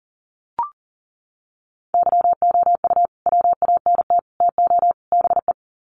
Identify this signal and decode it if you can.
Morse code, heard as a keyed tone, about 40 wpm, 700 Hz. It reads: YOU WANT TO BE
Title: Sets of 4 Words 40wpm